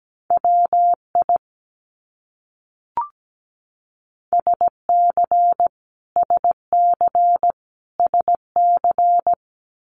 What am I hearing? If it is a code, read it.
Morse code, heard as a keyed tone, about 17 wpm, 700 Hz. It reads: WI SC SC SC
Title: US States Repeated 3x 17wpm